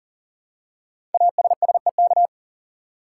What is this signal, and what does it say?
Morse code, heard as a keyed tone, about 41 wpm, 700 Hz. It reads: ASSEK
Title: Code Groups - 5 Letters - 40wpm